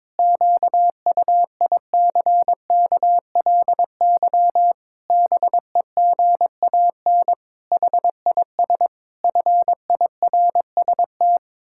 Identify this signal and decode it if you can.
Morse code, heard as a keyed tone, about 22 wpm, 700 Hz. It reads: QUICKLY BEGAN HIS FIRST
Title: Pangram Challenge v2 - 22wpm